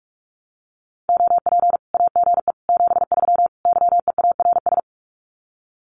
Morse code, heard as a keyed tone, about 44 wpm, 700 Hz. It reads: OP AGE 73 YEARS
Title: Common QSO Phrases 45wpm